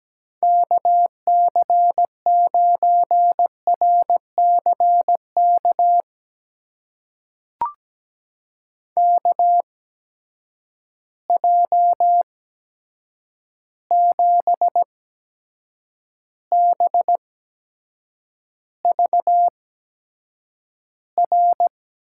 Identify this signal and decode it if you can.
Morse code, heard as a keyed tone, about 17 wpm, spaced as for 7 wpm, 700 Hz. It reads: KC9RCK KJ7BVR
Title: USA Call Signs - 6 Characters - 8x Character Spacing - 17wpm